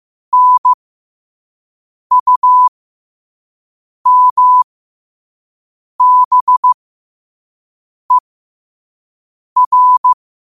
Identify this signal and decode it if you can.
Morse code, heard as a keyed tone, about 15 wpm, spaced as for 5 wpm, 1000 Hz. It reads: NUMBER
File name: qotdfiles / 5wpm